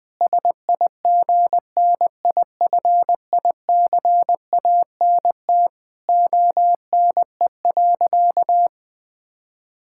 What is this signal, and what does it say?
Morse code, heard as a keyed tone, about 20 wpm, 700 Hz. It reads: SIGNIFICANT ONE.
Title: Sentences from Top 1000 Words - Review Part 6 - 20wpm